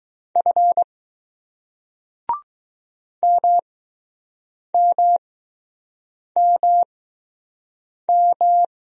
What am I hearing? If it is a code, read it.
Morse code, heard as a keyed tone, about 20 wpm, spaced as for 6 wpm, 700 Hz. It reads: F MMMM